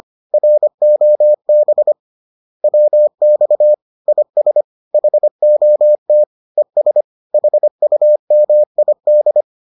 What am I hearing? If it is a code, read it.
Morse code, heard as a keyed tone, about 25 wpm, 600 Hz. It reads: ROB WX IS HOT ES HUMID